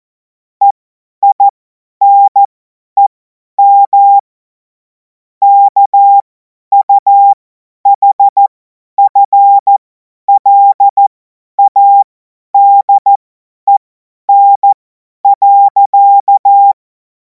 Morse code, heard as a keyed tone, about 14 wpm, spaced as for 10 wpm, 800 Hz. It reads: EINEM KUHFLADEN.
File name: Morsekurs / Witze / Witz 21. April 2026 50 F70